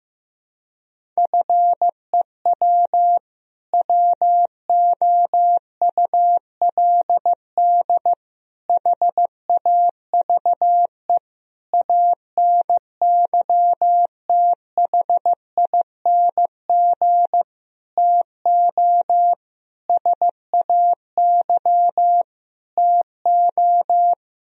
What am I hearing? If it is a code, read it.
Morse code, heard as a keyed tone, about 15 wpm, 700 Hz. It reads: FEW WOULD HAVE ANYTHING TO SAY TO
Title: Sentences from Top 1000 Words - Review Part 1 - 15wpm